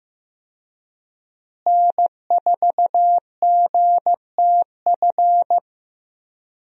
Morse code, heard as a keyed tone, about 15 wpm, 700 Hz. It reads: N4GTF
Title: USA Call Signs - 5 Characters - Repeated 2x - 15wpm